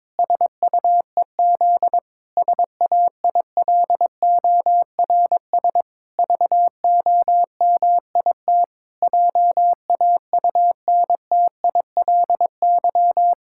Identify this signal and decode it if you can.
Morse code, heard as a keyed tone, about 22 wpm, 700 Hz. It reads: SUEZ SAILORS VOMIT JAUNTILY
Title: Pangram Challenge v2 - 22wpm